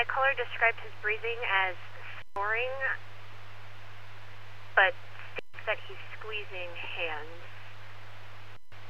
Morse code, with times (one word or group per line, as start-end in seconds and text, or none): none